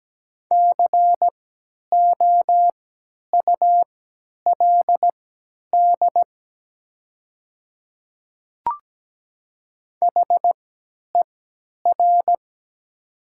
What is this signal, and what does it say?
Morse code, heard as a keyed tone, about 17 wpm, spaced as for 10 wpm, 690 Hz. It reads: COULD HER
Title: 3x Character Spacing - Top 100 Words - 17wpm